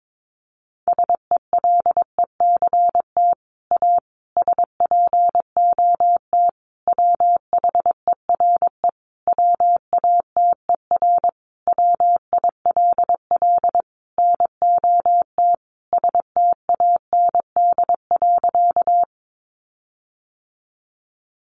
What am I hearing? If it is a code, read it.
Morse code, heard as a keyed tone, about 22 wpm, 700 Hz. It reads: SELECT A SPOT WHERE WATER WILL NOT STAND.
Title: Sentences from Top 1300 Words 22wpm